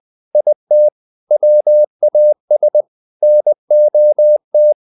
Morse code, heard as a keyed tone, about 20 wpm, 600 Hz. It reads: IT WAS NOT
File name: the old man and the sea / the old man and the sea part 1 20wpm